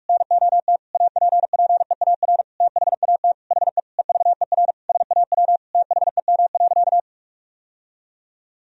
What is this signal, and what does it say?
Morse code, heard as a keyed tone, about 46 wpm, 700 Hz. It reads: NOT APPEAR THAT HE EVER SAW THEM.
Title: Sentences from Top 1000 Words - Review Part 2 - 45wpm